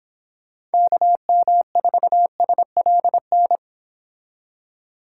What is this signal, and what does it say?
Morse code, heard as a keyed tone, about 29 wpm, 700 Hz. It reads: KM4SLN